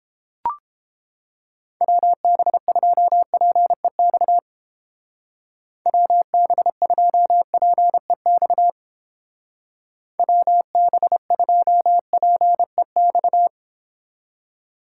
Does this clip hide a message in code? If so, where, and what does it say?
Yes: Morse code from 1.8 to 13.5 s, heard as a keyed tone, about 29 wpm, 700 Hz. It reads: WB2PEX WB2PEX WB2PEX